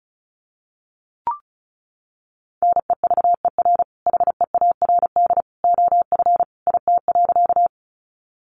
Morse code, heard as a keyed tone, about 34 wpm, 700 Hz. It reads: NEVER HEARD OF IT.
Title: Sentences from Top 400 Words 35wpm